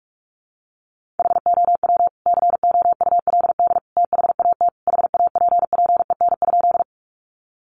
Morse code, heard as a keyed tone, about 44 wpm, 700 Hz. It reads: HOW COULD THAT HAPPEN?
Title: Sentences from Top 400 Words 45wpm